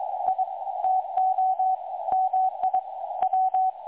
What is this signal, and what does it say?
Morse code, heard as a keyed tone, about 23 wpm, 750 Hz, with radio noise. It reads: I TO MI W
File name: dlazaawansowanych / nagraniazpasma / websdr recording 3565.0kHz